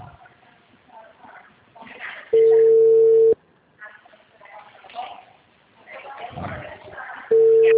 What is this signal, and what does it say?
Ringback tone (China, 450 Hz), with speech.
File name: out- -2025- - - .17383